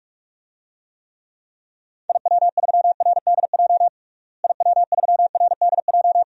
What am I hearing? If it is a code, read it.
Morse code, heard as a keyed tone, about 42 wpm, 700 Hz. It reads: IW3RDJ IW3RDJ
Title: DXCC-100 Call Signs - 6 Characters - Repeated 2x - 45wpm